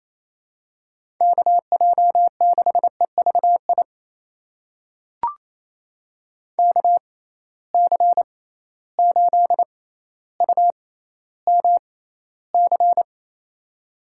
Morse code, heard as a keyed tone, about 28 wpm, spaced as for 14 wpm, 700 Hz. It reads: KJ6EVI KC8UMC
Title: USA Call Signs - 6 Characters - 6x Character Spacing - 28wpm